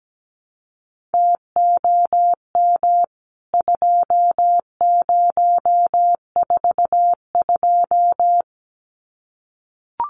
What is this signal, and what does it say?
Morse code, heard as a keyed tone, about 17 wpm, 700 Hz. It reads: TOM 2042
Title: CWT Exchange 17wpm